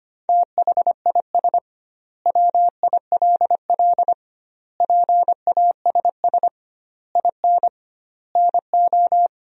Morse code, heard as a keyed tone, about 25 wpm, 700 Hz. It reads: THIS WILL PASS IN NO
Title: 2x Word-Spacing - Sentences from Top 400 Words - 25wpm